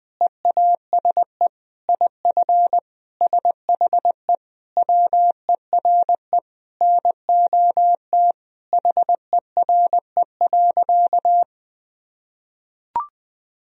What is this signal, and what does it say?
Morse code, heard as a keyed tone, about 21 wpm, 700 Hz. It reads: EASE IF SHE WERE NOT HERE.
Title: Sentences from Top 1000 Words - Review Part 1 - 20wpm